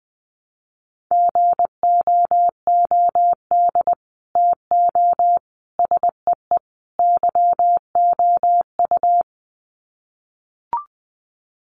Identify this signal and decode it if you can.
Morse code, heard as a keyed tone, about 20 wpm, 700 Hz. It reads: GOOD TO SEE YOU
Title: Sets of 4 Words 20wpm